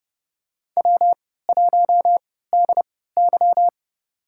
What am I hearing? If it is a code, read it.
Morse code, heard as a keyed tone, about 30 wpm, spaced as for 17 wpm, 700 Hz. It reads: W1DY
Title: USA Call Signs - 4 Characters - 3x Character Spacing - 30wpm